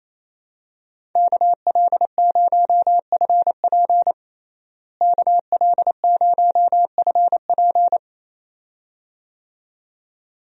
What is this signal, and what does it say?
Morse code, heard as a keyed tone, about 28 wpm, 700 Hz. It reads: KL0FP KL0FP
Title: USA Call Signs - 5 Characters - Repeated 2x - 28wpm